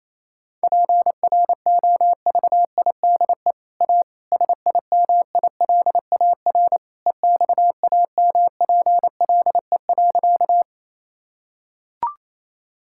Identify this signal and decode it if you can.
Morse code, heard as a keyed tone, about 28 wpm, 700 Hz. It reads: PROVIDE A SIMILAR EXAMPLE.
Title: Sentences from Top 1000 Words - Review Part 5 - 28wpm